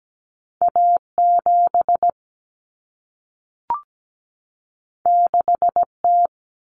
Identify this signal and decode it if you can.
Morse code, heard as a keyed tone, about 17 wpm, 700 Hz. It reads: A7 6T
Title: Common Cut Numbers - 2-Digit 17wpm